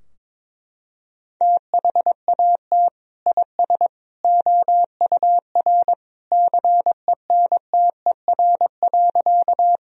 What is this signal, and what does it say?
Morse code, heard as a keyed tone, about 22 wpm, 700 Hz. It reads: THAT IS OUR CENTER.
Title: Sentences from Top 1000 Words - Review Part 2 - 22wpm